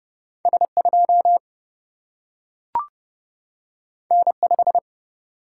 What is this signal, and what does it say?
Morse code, heard as a keyed tone, about 30 wpm, 700 Hz. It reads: S 2 N 5
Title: Single-Letter Single-Number 30wpm